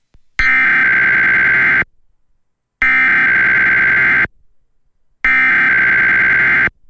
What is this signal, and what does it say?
Emergency Alert System (SAME) tones (header bursts), heard as harsh buzzing bursts. It is heard from 0.4 to 6.7 s.